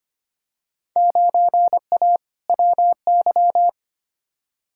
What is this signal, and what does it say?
Morse code, heard as a keyed tone, about 25 wpm, 700 Hz. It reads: 9A WY
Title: Field Day Exchanges Encore - 25wpm